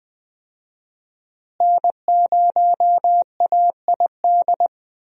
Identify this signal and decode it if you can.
Morse code, heard as a keyed tone, about 20 wpm, 700 Hz. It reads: N0AID
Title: USA Call Signs - 5 Characters - Repeated 2x - 20wpm